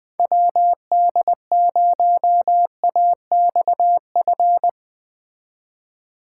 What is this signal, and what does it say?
Morse code, heard as a keyed tone, about 20 wpm, 700 Hz. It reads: WD0AXF